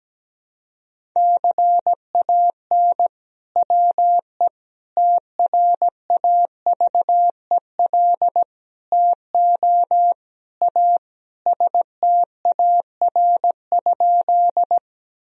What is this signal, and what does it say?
Morse code, heard as a keyed tone, about 17 wpm, 700 Hz. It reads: CAN WE TRAVEL TO A STAR?
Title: Sentences from Top 1000 Words - Review Part 2 - 17wpm